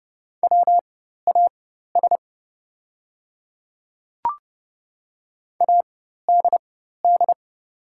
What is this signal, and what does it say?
Morse code, heard as a keyed tone, about 30 wpm, spaced as for 14 wpm, 700 Hz. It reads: WAS ADD
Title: 4x Character Spacing - Top 200 Words - 30wpm